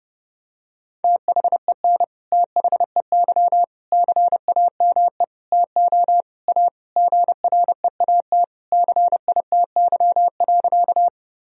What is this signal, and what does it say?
Morse code, heard as a keyed tone, about 32 wpm, 700 Hz. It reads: THEN THEY CAME TO A GREAT CITY.
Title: Sentences from Top 1000 Words - Review Part 1 - 30wpm